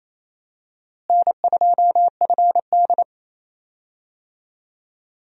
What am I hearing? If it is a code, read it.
Morse code, heard as a keyed tone, about 28 wpm, 700 Hz. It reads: N2FD
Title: USA Call Signs - 4 Characters - Repeated 3x - 28wpm